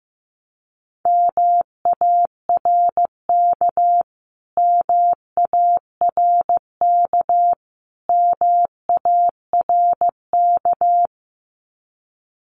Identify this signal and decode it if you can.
Morse code, heard as a keyed tone, about 15 wpm, 700 Hz. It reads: MARK MARK MARK
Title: Top 300 Words Repeated 15wpm